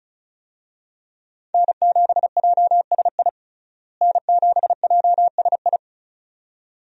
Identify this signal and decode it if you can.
Morse code, heard as a keyed tone, about 36 wpm, 700 Hz. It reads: N7JSI N7JSI
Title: USA Call Signs - 5 Characters - Repeated 2x - 35wpm